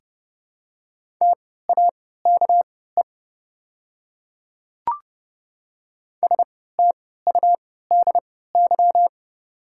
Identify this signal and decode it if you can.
Morse code, heard as a keyed tone, about 30 wpm, spaced as for 17 wpm, 700 Hz. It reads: TAKE STUDY
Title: 3x Character Spacing - Top 300 Words - 30wpm